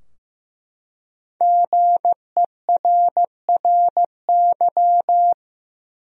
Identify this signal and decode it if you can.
Morse code, heard as a keyed tone, about 15 wpm, 700 Hz. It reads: GERRY
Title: USA Common Names - Max 5 Letters - 15wpm